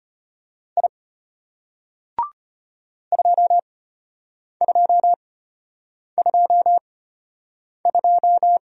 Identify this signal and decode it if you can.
Morse code, heard as a keyed tone, about 33 wpm, spaced as for 8 wpm, 700 Hz. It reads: I 2222